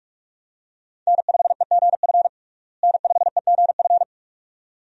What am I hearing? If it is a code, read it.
Morse code, heard as a keyed tone, about 44 wpm, 700 Hz. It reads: N5EGF N5EGF
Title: USA Call Signs - 5 Characters - Repeated 2x - 45wpm